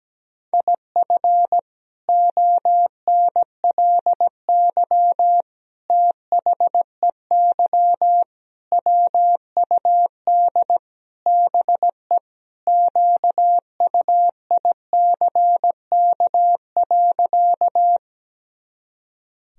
Morse code, heard as a keyed tone, about 17 wpm, 700 Hz. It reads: IF ONLY THEY WUD BE QUICK.
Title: Sentences from Top 500 Words and Abbreviations 17wpm